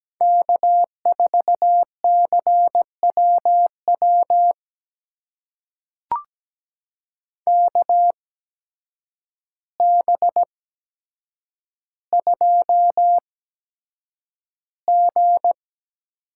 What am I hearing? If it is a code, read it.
Morse code, heard as a keyed tone, about 17 wpm, spaced as for 8 wpm, 700 Hz. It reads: K4CWW KB2G
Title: USA Call Signs - 8x Character Spacing - 17wpm